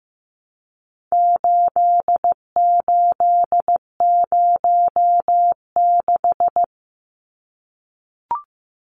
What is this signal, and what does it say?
Morse code, heard as a keyed tone, about 15 wpm, 700 Hz. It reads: 8806
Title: Numbers - 4-Digit 15wpm